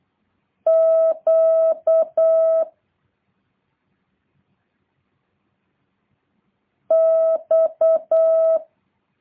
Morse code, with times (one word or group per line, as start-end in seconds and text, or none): 0.7-2.6 Q
6.9-8.6 X